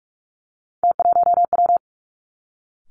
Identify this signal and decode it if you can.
Morse code, heard as a keyed tone, about 46 wpm, 700 Hz. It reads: T1W
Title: Three Challenging Letter-Number Combinations 45wpm